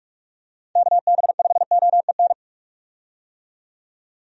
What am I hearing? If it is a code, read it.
Morse code, heard as a keyed tone, about 45 wpm, 700 Hz. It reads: KB5OEN
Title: USA Call Signs - 6 Characters - 45wpm